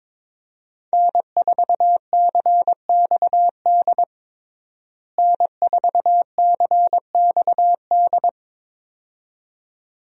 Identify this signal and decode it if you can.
Morse code, heard as a keyed tone, about 22 wpm, 700 Hz. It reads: N4CXD N4CXD
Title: USA Call Signs - 5 Characters - Repeated 2x - 22wpm